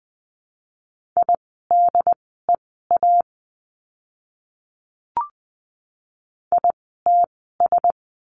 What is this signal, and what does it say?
Morse code, heard as a keyed tone, about 20 wpm, spaced as for 15 wpm, 700 Hz. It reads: IDEA ITS
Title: 108 Lesson 18f - Letter m - Words 20wpm